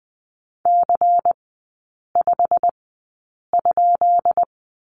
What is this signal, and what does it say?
Morse code, heard as a keyed tone, about 20 wpm, spaced as for 8 wpm, 690 Hz. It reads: C5?
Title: 133 Lesson 22e - Question Mark - Three Characters 20wpm